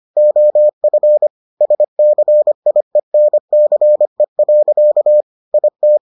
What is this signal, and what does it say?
Morse code, heard as a keyed tone, about 25 wpm, 600 Hz. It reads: OF SCIENCE. IT